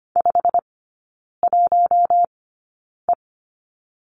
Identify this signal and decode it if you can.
Morse code, heard as a keyed tone, about 25 wpm, spaced as for 9 wpm, 700 Hz. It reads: 51E
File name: Lesson 15e - Number 5 - Three Characters 25wpm